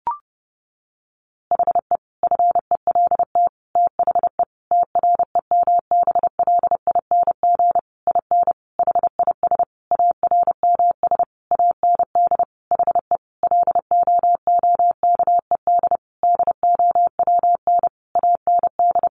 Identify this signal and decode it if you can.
Morse code, heard as a keyed tone, about 28 wpm, 700 Hz. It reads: HE FELT THE TREMBLING IN HIS ARMS AND HE LOOKED DOWN AND
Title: A Sound Of Thunder 30wpm